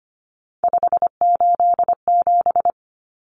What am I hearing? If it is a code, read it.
Morse code, heard as a keyed tone, about 25 wpm, 700 Hz. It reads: 587
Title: Numbers - 3-Digit 25wpm